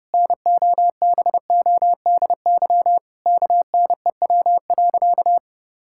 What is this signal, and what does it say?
Morse code, heard as a keyed tone, about 31 wpm, 700 Hz. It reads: NOBODY KNEW.
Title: Sentences from Top 1600 Words 30wpm